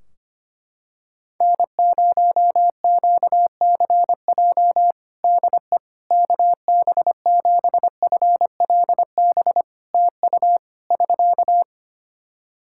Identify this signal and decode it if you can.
Morse code, heard as a keyed tone, about 25 wpm, 700 Hz. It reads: N0QCJ DE KB7FLB TU <SK>
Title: Practice QSO - Exchange 4 - Ending - 25wpm